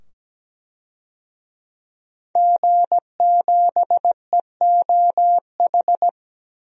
Morse code, heard as a keyed tone, about 17 wpm, 700 Hz. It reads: G7EOH
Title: DXCC-100 Call Signs - 5 Characters - 17wpm